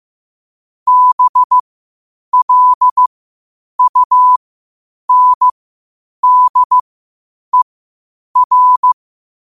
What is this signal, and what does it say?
Morse code, heard as a keyed tone, about 15 wpm, spaced as for 8 wpm, 1000 Hz. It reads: BLUNDER
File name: 8wpm